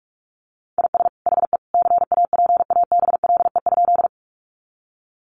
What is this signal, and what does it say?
Morse code, heard as a keyed tone, about 44 wpm, 700 Hz. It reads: IS HE CAPABLE?
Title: Sentences from Top 1200 Words 45wpm